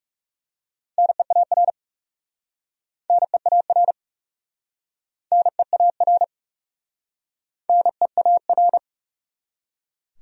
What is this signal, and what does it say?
Morse code, heard as a keyed tone, about 41 wpm, 700 Hz. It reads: NEAR NEAR NEAR NEAR